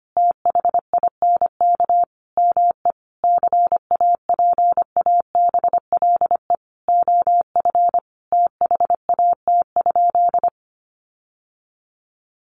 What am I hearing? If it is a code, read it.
Morse code, heard as a keyed tone, about 25 wpm, 700 Hz. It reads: THINK ME CAPABLE OF THAT?
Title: Sentences from Top 1200 Words 25wpm